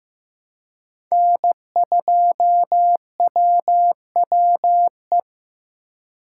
Morse code, heard as a keyed tone, about 15 wpm, 700 Hz. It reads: N2WWE
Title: USA Call Signs - 5 Characters - 3x Character Spacing - 15wpm